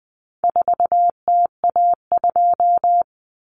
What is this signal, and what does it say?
Morse code, heard as a keyed tone, about 20 wpm, 700 Hz. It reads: 4TA2
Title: Common Cut Numbers - 4-Digit 20wpm